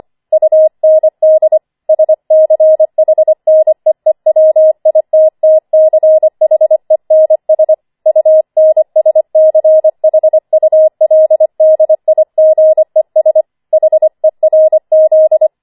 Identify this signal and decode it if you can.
Morse code, heard as a keyed tone, about 25 wpm, 630 Hz. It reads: UND SCHNEEWITTCHENS UNSCHULDIGES HERZ
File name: 25wpm